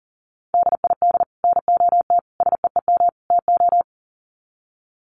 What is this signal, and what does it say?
Morse code, heard as a keyed tone, about 40 wpm, 700 Hz. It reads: DID NOT SEEM TO
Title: Sets of 4 Words 40wpm